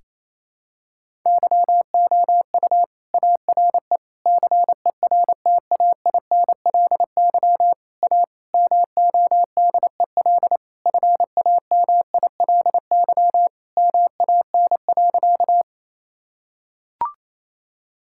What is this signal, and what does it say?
Morse code, heard as a keyed tone, about 28 wpm, 700 Hz. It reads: YOU ARE CERTAINLY A MODEL FAMILY MAN.
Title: Sentences from Top 1000 Words - Review Part 5 - 28wpm